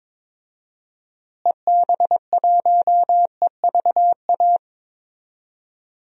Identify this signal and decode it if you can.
Morse code, heard as a keyed tone, about 22 wpm, 700 Hz. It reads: EB1EVA
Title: DXCC-100 Call Signs - 6 Characters - Repeated 2x - 22wpm